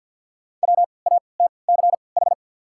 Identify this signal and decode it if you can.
Morse code, heard as a keyed tone, about 53 wpm, spaced as for 29 wpm, 700 Hz. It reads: WATCH
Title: 3x Character Spacing - Top 500 Words - 50wpm